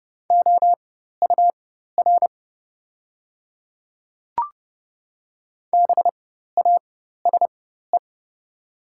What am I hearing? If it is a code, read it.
Morse code, heard as a keyed tone, about 30 wpm, spaced as for 14 wpm, 690 Hz. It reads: OUR BASE
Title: 4x Character Spacing - Top 500 Words - 30wpm